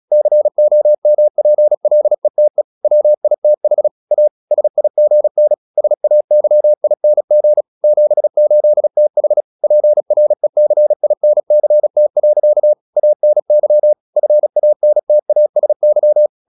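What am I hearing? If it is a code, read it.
Morse code, heard as a keyed tone, about 36 wpm, 600 Hz. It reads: COMPLETE WITH A SIGN SAYING 78TH PRECINCT. ANY FANTASY
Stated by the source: Morse practice at 36 wpm, text of Ken Follett's novel Never, from CW med Gnister.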